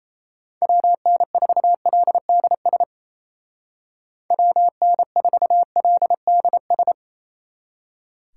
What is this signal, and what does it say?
Morse code, heard as a keyed tone, about 34 wpm, 700 Hz. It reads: WN4LDS WN4LDS
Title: Speed Racing - USA Call Signs 28wpm